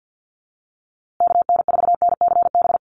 Morse code, heard as a keyed tone, about 50 wpm, 700 Hz. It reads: KN4NCB
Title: USA Call Signs Repeated 3x 50wpm